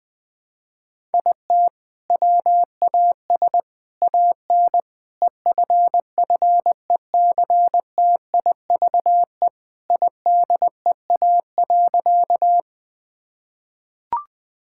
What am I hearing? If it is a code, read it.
Morse code, heard as a keyed tone, about 21 wpm, 700 Hz. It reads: IT WAS AN EFFECTIVE IDEA.
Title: Sentences from Top 1000 Words - Review Part 6 - 20wpm